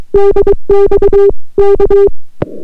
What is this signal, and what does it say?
Morse code, heard as a keyed tone, about 23 wpm, 400 Hz. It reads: DX K